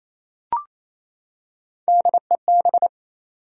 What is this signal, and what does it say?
Morse code, heard as a keyed tone, about 28 wpm, 700 Hz. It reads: DEB
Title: USA Common Names - Max 3 Letters - 28wpm